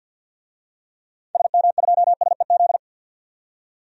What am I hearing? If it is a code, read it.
Morse code, heard as a keyed tone, about 50 wpm, 700 Hz. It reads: SM2SEZ
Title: DXCC-100 Call Signs - 6 Characters - 50wpm